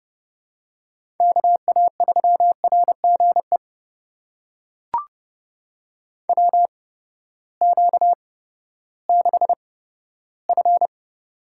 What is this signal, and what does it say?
Morse code, heard as a keyed tone, about 29 wpm, spaced as for 15 wpm, 700 Hz. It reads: KA3RGE WQ6F
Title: USA Call Signs - 8x Character Spacing - 30wpm